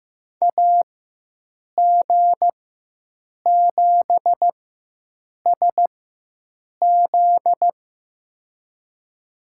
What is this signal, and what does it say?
Morse code, heard as a keyed tone, about 15 wpm, spaced as for 7 wpm, 700 Hz. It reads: AG7SZ
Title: USA Call Signs - 5 Characters - 4x Character Spacing - 15wpm